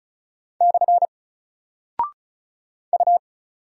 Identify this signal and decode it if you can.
Morse code, heard as a keyed tone, about 36 wpm, 700 Hz. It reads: / U
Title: Single Character (Letter:Number:Punctuation:Prosign) 35wpm